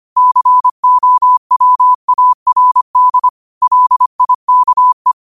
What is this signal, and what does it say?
Morse code, heard as a keyed tone, about 25 wpm, 1000 Hz. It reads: COWARD LIKE
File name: qotdfiles / 25wpm